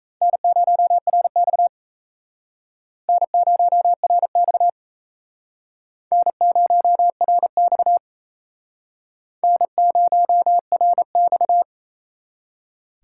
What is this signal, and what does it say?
Morse code, heard as a keyed tone, about 36 wpm, 700 Hz. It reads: N0RX N0RX N0RX N0RX